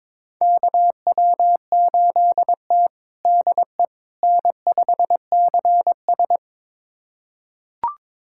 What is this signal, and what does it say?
Morse code, heard as a keyed tone, about 22 wpm, 700 Hz. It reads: KW8T DE N5CS
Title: USA Callsign DE Callsign - 4 Characters - 22wpm